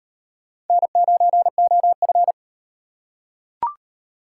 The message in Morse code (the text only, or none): N9OF